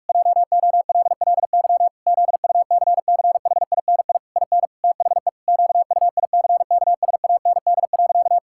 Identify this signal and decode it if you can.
Morse code, heard as a keyed tone, about 43 wpm, 700 Hz. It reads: JOLLY ZUCCHINI IN THE QUICKSAND.
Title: Pangram Challenge v2 - 45wpm